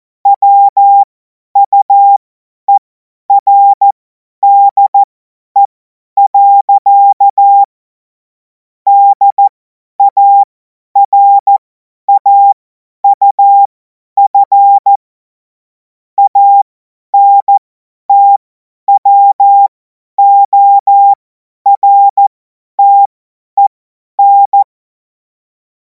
Morse code, heard as a keyed tone, about 14 wpm, spaced as for 10 wpm, 800 Hz. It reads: WUERDE. DARAUF ANTWORTEN
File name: Morsekurs / Witze / Witz 07. April 2026 50 F70